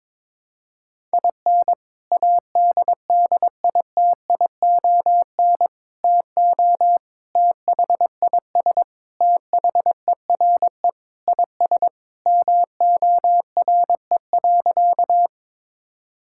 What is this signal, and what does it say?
Morse code, heard as a keyed tone, about 22 wpm, 700 Hz. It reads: IN ADDITION TO THIS THERE IS MORE.
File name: practice / 22wpm / Sentences with 3-Word Phrases - 22wpm